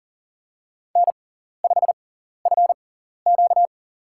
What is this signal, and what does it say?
Morse code, heard as a keyed tone, about 39 wpm, spaced as for 14 wpm, 700 Hz. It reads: N5FQ
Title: USA Call Signs - 6x Character Spacing - 40wpm